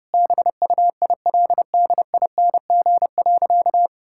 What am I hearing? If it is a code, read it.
Morse code, heard as a keyed tone, about 32 wpm, 700 Hz. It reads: BUILDING.
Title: Sentences from Top 1000 Words - Review Part 5 - 30wpm